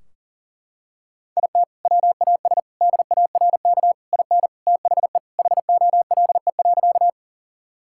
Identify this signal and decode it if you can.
Morse code, heard as a keyed tone, about 37 wpm, 700 Hz. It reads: IT WAS DARK IN THE HOLE.
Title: Sentences from Top 1600 Words 40wpm